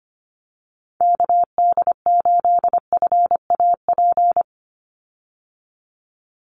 Morse code, heard as a keyed tone, about 25 wpm, 700 Hz. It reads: KD8FAP
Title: USA Call Signs 25wpm